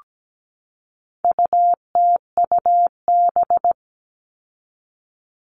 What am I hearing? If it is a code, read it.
Morse code, heard as a keyed tone, about 17 wpm, 700 Hz. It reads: UTUB
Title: All Cut Numbers - 4-Digit 17wpm